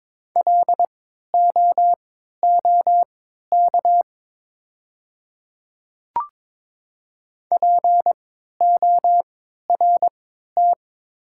Morse code, heard as a keyed tone, about 22 wpm, spaced as for 12 wpm, 700 Hz. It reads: LOOK PORT
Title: 3x Character Spacing - Top 200 Words - 22wpm